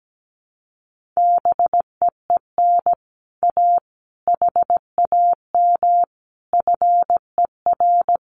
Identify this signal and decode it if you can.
Morse code, heard as a keyed tone, about 17 wpm, 700 Hz. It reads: BEEN A HAM FER
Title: Common QSO Phrases Encore 17wpm